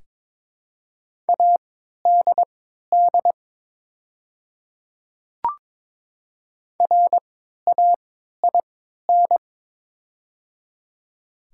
Morse code, heard as a keyed tone, about 22 wpm, spaced as for 13 wpm, 690 Hz. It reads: ADD RAIN
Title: 3x Character Spacing - Top 500 Words - 22wpm